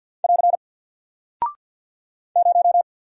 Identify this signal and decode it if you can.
Morse code, heard as a keyed tone, about 49 wpm, 700 Hz. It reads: <AR> 0
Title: Single Character (Letter:Number:Punctuation:Prosign) Alt 50wpm